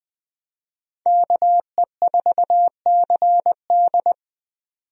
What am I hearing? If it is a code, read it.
Morse code, heard as a keyed tone, about 20 wpm, 700 Hz. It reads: KE4CD
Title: USA Call Signs - 5 Characters - 20wpm